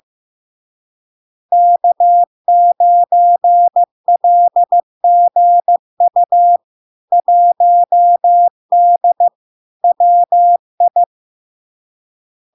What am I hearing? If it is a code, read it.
Morse code, heard as a keyed tone, about 15 wpm, 700 Hz. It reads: K9LGU 1D WI
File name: LARK-FD-2022-Log-in-CW-15wpm-700-Hz